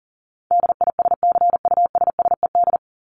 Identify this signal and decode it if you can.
Morse code, heard as a keyed tone, about 40 wpm, 700 Hz. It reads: DISCUSSED
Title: ED Suffix - Top 100 Words with Base Words 40wpm